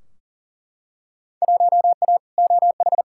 Morse code, heard as a keyed tone, about 40 wpm, 700 Hz. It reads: 1A OH
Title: Field Day Exchanges - 40wpm